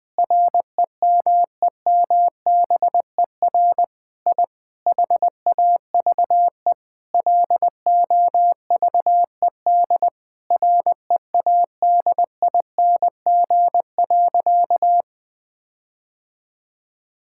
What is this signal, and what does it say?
Morse code, heard as a keyed tone, about 21 wpm, 700 Hz. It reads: REMEMBER I HAVE LOVED READING.
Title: Sentences with 8-Word Phrases - 20wpm